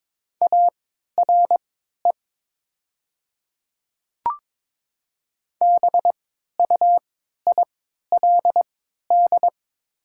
Morse code, heard as a keyed tone, about 21 wpm, spaced as for 12 wpm, 700 Hz. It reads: ARE BUILD
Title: 3x Character Spacing - Top 200 Words - 22wpm